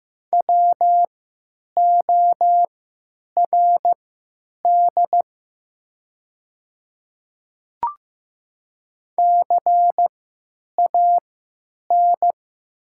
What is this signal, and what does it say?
Morse code, heard as a keyed tone, about 15 wpm, spaced as for 9 wpm, 700 Hz. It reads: WORD CAN
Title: 3x Character Spacing - Top 100 Words - 15wpm